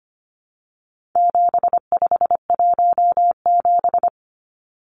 Morse code, heard as a keyed tone, about 25 wpm, 700 Hz. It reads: 7517
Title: Numbers - 4-Digit 25wpm